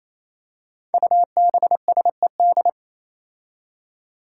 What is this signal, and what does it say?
Morse code, heard as a keyed tone, about 28 wpm, 700 Hz. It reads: UBSED
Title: Five Challenging Letter-Number Combinations 28wpm